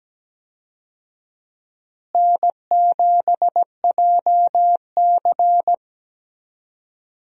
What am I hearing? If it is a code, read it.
Morse code, heard as a keyed tone, about 17 wpm, 700 Hz. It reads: N7JC
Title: USA Call Signs - 4 Characters - Repeated 2x - 17wpm